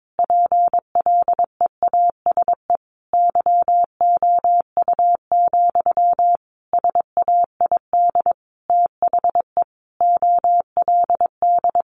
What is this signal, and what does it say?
Morse code, heard as a keyed tone, about 22 wpm, 700 Hz. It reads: PLEASE YOU, SAID THE OLD
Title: Sentences from Top 1500 Words 22wpm